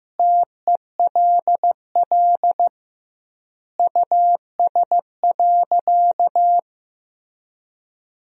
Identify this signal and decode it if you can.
Morse code, heard as a keyed tone, about 15 wpm, 700 Hz. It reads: TELL US.
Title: 2x Word-Spacing - Sentences from Top 400 Words - 15wpm